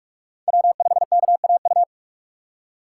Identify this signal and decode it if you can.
Morse code, heard as a keyed tone, about 46 wpm, 700 Hz. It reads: W5KAU
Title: USA Call Signs - 2x Character Spacing - 45wpm